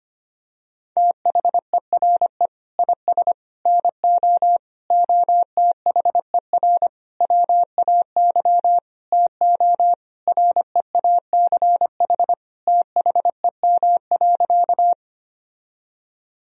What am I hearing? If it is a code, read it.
Morse code, heard as a keyed tone, about 25 wpm, 700 Hz. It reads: THERE IS NO OTHER WAY TO REACH THEM.
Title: Sentences from Top 1000 Words - Review Part 2 - 25wpm